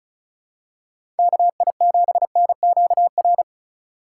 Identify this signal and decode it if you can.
Morse code, heard as a keyed tone, about 34 wpm, 700 Hz. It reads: KI7NQR
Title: USA Call Signs - 6 Characters - 2x Character Spacing - 35wpm